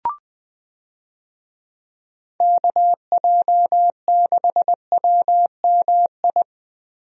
Morse code, heard as a keyed tone, about 20 wpm, 700 Hz. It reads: KJ6WMI
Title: USA Call Signs - 6 Characters - Repeated 3x - 20wpm